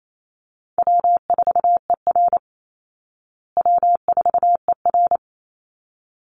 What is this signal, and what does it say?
Morse code, heard as a keyed tone, about 28 wpm, 700 Hz. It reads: W4ER W4ER
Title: Call Signs Repeated 3x - 28wpm